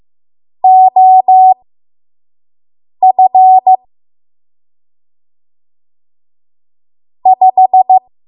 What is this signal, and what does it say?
Morse code, heard as a keyed tone, about 15 wpm, spaced as for 5 wpm, 750 Hz. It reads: OF 5
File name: Morse / 05WPM